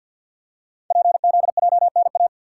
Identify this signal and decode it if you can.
Morse code, heard as a keyed tone, about 51 wpm, 700 Hz. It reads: PZJNA
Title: Code Groups - 5 Letters - 50wpm